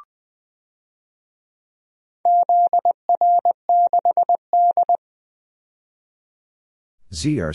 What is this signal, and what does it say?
Morse code, heard as a keyed tone, about 20 wpm, 700 Hz. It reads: ZR6D
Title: DXCC-100 Call Signs - 4 Characters - 20wpm